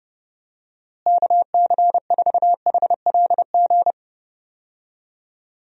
Morse code, heard as a keyed tone, about 30 wpm, 700 Hz. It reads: KC4HLG
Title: USA Call Signs - 6 Characters - Repeated 3x - 30wpm